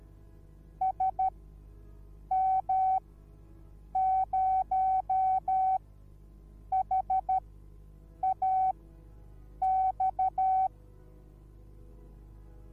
Morse code, heard as a keyed tone, about 13 wpm, spaced as for 7 wpm, 760 Hz. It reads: SM0HAX